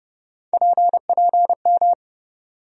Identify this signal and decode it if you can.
Morse code, heard as a keyed tone, about 30 wpm, 700 Hz. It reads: PPM
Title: USA Call Sign Postfixes - 3 Letters 30wpm